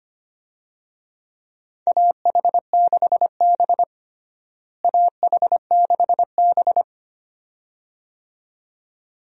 Morse code, heard as a keyed tone, about 25 wpm, 700 Hz. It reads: AH6B AH6B
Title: USA Call Signs - 4 Characters - Repeated 2x - 25wpm